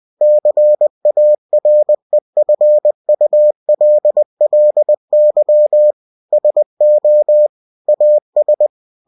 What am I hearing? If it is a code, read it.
Morse code, heard as a keyed tone, about 20 wpm, 600 Hz. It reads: CAREFULLY SO AS
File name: cw-books / winnie-the-pooh / winnie-20wpm-page-0036